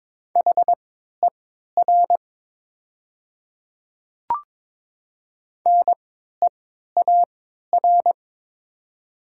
Morse code, heard as a keyed tone, about 22 wpm, spaced as for 12 wpm, 700 Hz. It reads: HER NEAR
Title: 3x Character Spacing - Top 200 Words - 22wpm